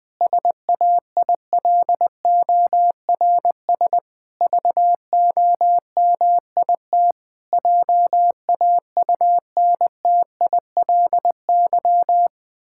Morse code, heard as a keyed tone, about 21 wpm, 700 Hz. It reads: SAILORS VOMIT JAUNTILY
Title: Pangram Challenge v2 - 20wpm